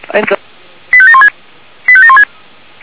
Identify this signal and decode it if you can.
Five-tone selective call (ZVEI), heard as quick runs of five tones, tones of about 70 ms, with speech. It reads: 85716 85716